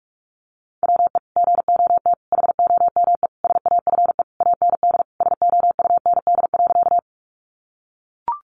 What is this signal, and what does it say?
Morse code, heard as a keyed tone, about 44 wpm, 700 Hz. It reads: WE GOT HOME SAFE AND SOUND.
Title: Sentences from Top 900 Words 45wpm